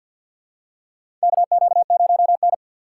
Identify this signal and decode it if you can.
Morse code, heard as a keyed tone, about 50 wpm, 700 Hz. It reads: KQ0N
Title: USA Call Signs - 4 Characters - 3x Character Spacing - 50wpm